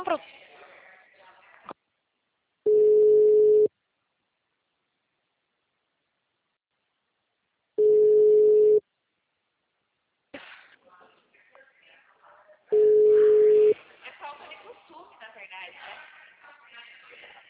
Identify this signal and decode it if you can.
Ringback tone (Europe, 425 Hz), with speech.
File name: out- -2011- - - .2